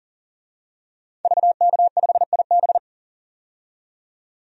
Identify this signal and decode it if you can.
Morse code, heard as a keyed tone, about 40 wpm, 700 Hz. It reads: VK5IB
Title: DXCC-100 Call Signs - 5 Characters - 40wpm